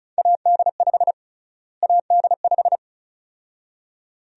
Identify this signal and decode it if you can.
Morse code, heard as a keyed tone, about 36 wpm, 700 Hz. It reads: AD5 AD5
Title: USA Call Sign Prefixes - Repeated 2x - 35wpm